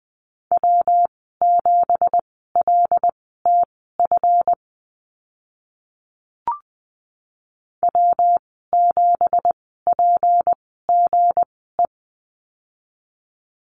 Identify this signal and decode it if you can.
Morse code, heard as a keyed tone, about 20 wpm, spaced as for 15 wpm, 700 Hz. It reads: W7LTF W7PGE
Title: 173 Lesson 28g - Number 9 - Call Signs 20wpm